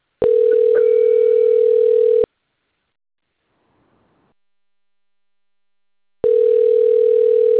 Ringback tone (North America), 440 + 480 Hz.